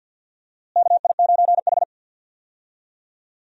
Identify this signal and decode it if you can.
Morse code, heard as a keyed tone, about 47 wpm, 700 Hz. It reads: KI9H
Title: USA Call Signs - 4 Characters - Repeated 2x - 50wpm